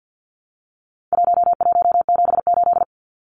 Morse code, heard as a keyed tone, about 50 wpm, 700 Hz. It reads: 1178
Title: Numbers - 4-Digit 50wpm